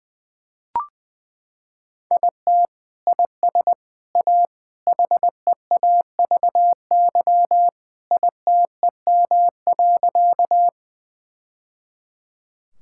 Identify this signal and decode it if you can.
Morse code, heard as a keyed tone, about 20 wpm, 700 Hz. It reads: IT IS A HEAVY ITEM.
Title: Sentences from Top 1000 Words - Review Part 6 - 20wpm